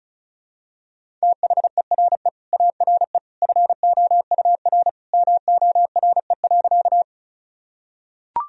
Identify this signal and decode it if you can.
Morse code, heard as a keyed tone, about 34 wpm, 700 Hz. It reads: THERE ARE FOUR MORE.
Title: Sentences from Top 1000 Words - Review Part 1 - 35wpm